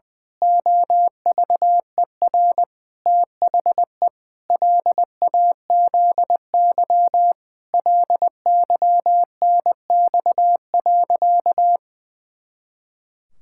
Morse code, heard as a keyed tone, about 19 wpm, 700 Hz. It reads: OVER THE LAZY LYNX.
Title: Pangram Challenge v2 - 20wpm